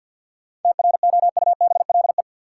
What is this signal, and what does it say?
Morse code, heard as a keyed tone, about 52 wpm, 700 Hz. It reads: TROUBLE
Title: Sentences from Top 1200 Words - New Word Drill - 50wpm